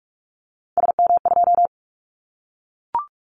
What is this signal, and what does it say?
Morse code, heard as a keyed tone, about 44 wpm, 700 Hz. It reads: SM2
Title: Callsign Prefixes for Top 25 DXCC - Repeated 3x - 45wpm